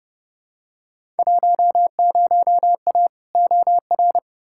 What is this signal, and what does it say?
Morse code, heard as a keyed tone, about 32 wpm, 700 Hz. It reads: 10A OR
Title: Field Day Exchanges - 30wpm